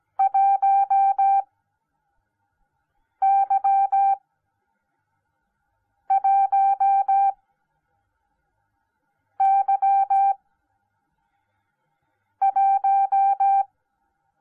Morse code, heard as a keyed tone, about 17 wpm, spaced as for 4 wpm, 790 Hz. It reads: 1Y1Y1